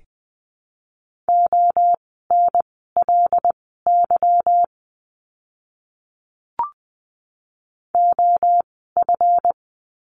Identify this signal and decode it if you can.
Morse code, heard as a keyed tone, about 20 wpm, spaced as for 15 wpm, 700 Hz. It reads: ONLY OF
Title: 160 Lesson 26f - Letter g - Words 20wpm